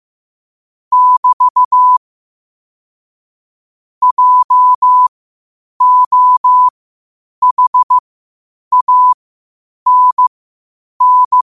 Morse code, heard as a keyed tone, about 15 wpm, spaced as for 8 wpm, 1000 Hz. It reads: <BT> JOHANN
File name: qotdfiles / 8wpm